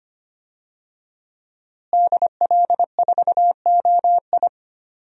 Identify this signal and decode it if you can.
Morse code, heard as a keyed tone, about 25 wpm, 700 Hz. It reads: DL4OI
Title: USA Call Signs - 5 Characters - Repeated 2x - 25wpm